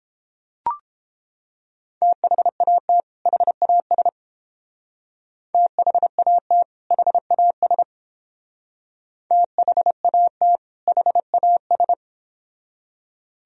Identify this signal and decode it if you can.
Morse code, heard as a keyed tone, about 31 wpm, 700 Hz. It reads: THAT HAS THAT HAS THAT HAS